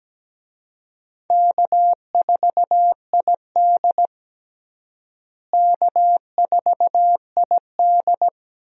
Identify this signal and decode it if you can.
Morse code, heard as a keyed tone, about 17 wpm, 700 Hz. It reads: K4ID K4ID
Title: USA Call Signs - 4 Characters - Repeated 2x - 17wpm